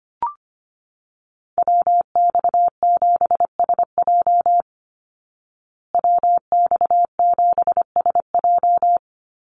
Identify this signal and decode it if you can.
Morse code, heard as a keyed tone, about 25 wpm, 700 Hz. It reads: WX7SJ WX7SJ
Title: Call Signs Repeated 2x - 25wpm